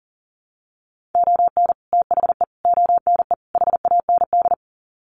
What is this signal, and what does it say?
Morse code, heard as a keyed tone, about 40 wpm, 700 Hz. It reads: ON THE ONE HAND
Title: Sets of 4 Words 40wpm